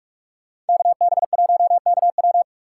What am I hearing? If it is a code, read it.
Morse code, heard as a keyed tone, about 46 wpm, 700 Hz. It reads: KB1KW